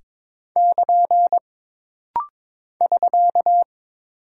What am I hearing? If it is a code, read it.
Morse code, heard as a keyed tone, about 22 wpm, 700 Hz. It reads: <KN> <SK>
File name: practice / 22wpm / Single Punctuation-Prosign - Rapid-Fire - 22wpm